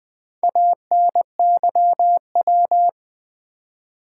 Morse code, heard as a keyed tone, about 20 wpm, 690 Hz. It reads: ANYW
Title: Build A Word - 2 Letters At A Time - 6-Letter Words - 20wpm